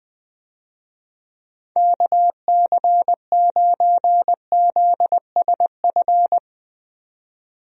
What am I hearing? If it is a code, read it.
Morse code, heard as a keyed tone, about 20 wpm, 700 Hz. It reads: KC9ZSF
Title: USA Call Signs - 6 Characters - 20wpm